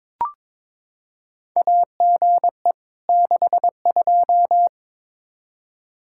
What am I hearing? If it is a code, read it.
Morse code, heard as a keyed tone, about 22 wpm, 700 Hz. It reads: AGE 62
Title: K1USN SST QSO Practice 22wpm